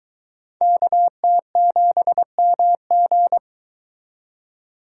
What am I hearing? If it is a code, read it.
Morse code, heard as a keyed tone, about 22 wpm, 700 Hz. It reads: KT7MG